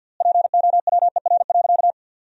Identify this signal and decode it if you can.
Morse code, heard as a keyed tone, about 51 wpm, 700 Hz. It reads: POWER.
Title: 4x Word-Spacing - Sentences from Top 500 Words - 50wpm